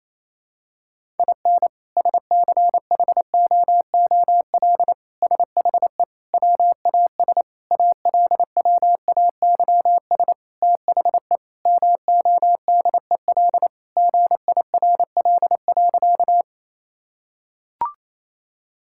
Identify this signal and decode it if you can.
Morse code, heard as a keyed tone, about 28 wpm, 700 Hz. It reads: IN SCHOOL SHE WAS ALWAYS THE MODEL GIRL.
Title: Sentences from Top 1000 Words - Review Part 5 - 28wpm